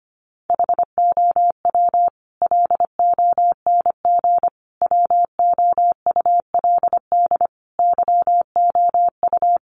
Morse code, heard as a keyed tone, about 25 wpm, 700 Hz. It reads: HOW LONG WOULD YOU
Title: Sentences from Top 500 Words 25wpm